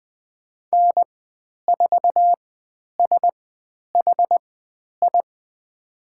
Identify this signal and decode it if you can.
Morse code, heard as a keyed tone, about 20 wpm, spaced as for 10 wpm, 700 Hz. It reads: N4SHI
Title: 109 Lesson 18g - Letter m - Call Signs 20wpm